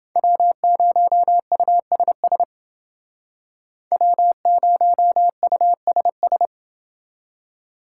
Morse code, heard as a keyed tone, about 28 wpm, 700 Hz. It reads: W0USS W0USS